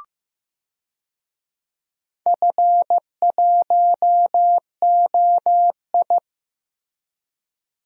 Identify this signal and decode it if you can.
Morse code, heard as a keyed tone, about 16 wpm, 700 Hz. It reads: F1OI
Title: DXCC-100 Call Signs - 4 Characters - Repeated 3x - 15wpm